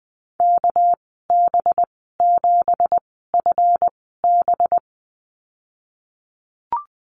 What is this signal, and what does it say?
Morse code, heard as a keyed tone, about 20 wpm, spaced as for 14 wpm, 700 Hz. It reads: KB7FB
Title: 219 Lesson 35g - Number 0 - Call Signs 20wpm